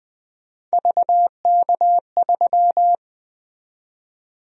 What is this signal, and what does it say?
Morse code, heard as a keyed tone, about 20 wpm, 700 Hz. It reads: VK3
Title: USA Call Sign Prefixes - 20wpm